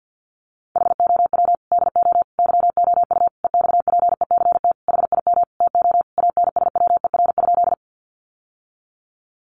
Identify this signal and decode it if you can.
Morse code, heard as a keyed tone, about 50 wpm, 700 Hz. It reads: HOW DO YOU EXPECT HIM TO ANSWER?
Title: Sentences from Top 1100 Words 50wpm